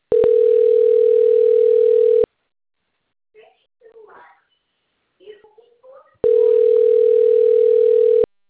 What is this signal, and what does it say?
Ringback tone (North America), 440 + 480 Hz, with speech.
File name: rg-1600-unknown- - - .21933